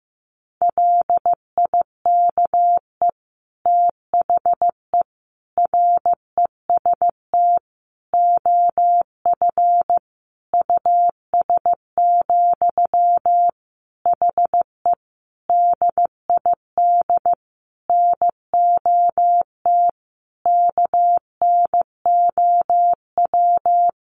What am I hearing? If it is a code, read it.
Morse code, heard as a keyed tone, about 14 wpm, 700 Hz. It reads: LIKE THE REST OF US, HE DID NOT KNOW
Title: Sentences from Top 1400 Words 15wpm